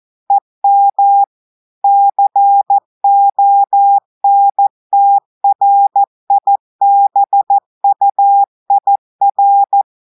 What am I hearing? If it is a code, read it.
Morse code, heard as a keyed tone, about 14 wpm, 800 Hz. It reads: EM CONTRIBUIR
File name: audios / portugais / portugais texte en clair 1 vitesse 14 mots minutes